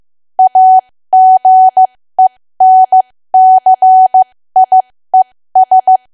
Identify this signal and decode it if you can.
Morse code, heard as a keyed tone, about 15 wpm, spaced as for 13 wpm, 750 Hz. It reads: AGENCIES
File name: cwa / W1AW-Level2 / 13WPM